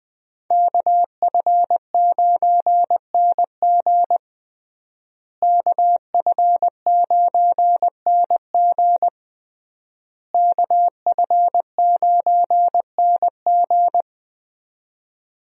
Morse code, heard as a keyed tone, about 21 wpm, 700 Hz. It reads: KF9NG KF9NG KF9NG
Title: USA Call Signs - 5 Characters - Repeated 3x - 20wpm